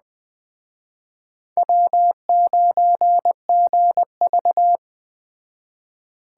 Morse code, heard as a keyed tone, about 22 wpm, 700 Hz. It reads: W9GV